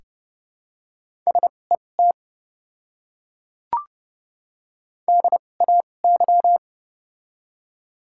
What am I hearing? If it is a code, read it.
Morse code, heard as a keyed tone, about 30 wpm, spaced as for 22 wpm, 700 Hz. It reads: SET DAY
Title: 2x Character Spacing - Top 500 Words - 30wpm